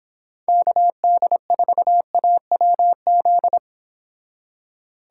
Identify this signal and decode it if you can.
Morse code, heard as a keyed tone, about 26 wpm, 700 Hz. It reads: KD4AWZ
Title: Speed Racing - USA 2x3 Call Signs 22wpm